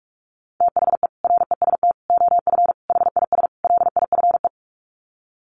Morse code, heard as a keyed tone, about 44 wpm, 700 Hz. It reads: THE REST OF HIS LIFE
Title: Sets of 5 Words 45wpm